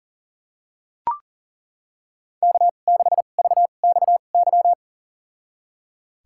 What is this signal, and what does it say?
Morse code, heard as a keyed tone, about 37 wpm, spaced as for 28 wpm, 700 Hz. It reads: K6VXY
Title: USA Call Signs - 5 Characters - 2x Character Spacing - 40wpm